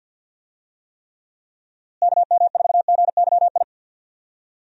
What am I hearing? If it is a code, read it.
Morse code, heard as a keyed tone, about 52 wpm, 700 Hz. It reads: KM4GYI
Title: USA Call Signs - 6 Characters - Repeated 2x - 50wpm